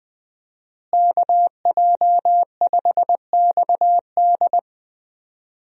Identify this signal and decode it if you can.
Morse code, heard as a keyed tone, about 20 wpm, 700 Hz. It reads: KJ5XD
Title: USA Call Signs - 5 Characters - Repeated 2x - 20wpm